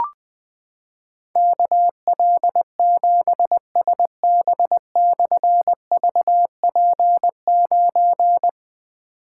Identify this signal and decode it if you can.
Morse code, heard as a keyed tone, about 20 wpm, 700 Hz. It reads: KL7SB/VP9
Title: Difficult Call Signs 20wpm